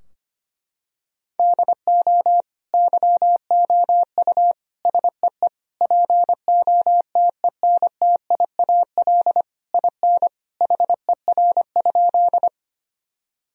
Morse code, heard as a keyed tone, about 25 wpm, 700 Hz. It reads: DO YOU SEE POTENTIAL IN HER?
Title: Sentences from Top 1000 Words - Review Part 4 - 25wpm